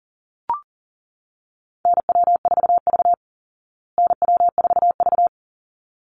Morse code, heard as a keyed tone, about 40 wpm, 700 Hz. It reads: NW4V NW4V
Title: Call Signs Repeated 3x - 40wpm